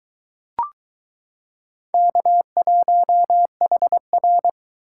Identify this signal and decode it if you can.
Morse code, heard as a keyed tone, about 22 wpm, 700 Hz. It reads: K1HR